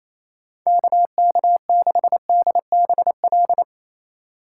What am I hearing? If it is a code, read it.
Morse code, heard as a keyed tone, about 35 wpm, 700 Hz. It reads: KK6DBL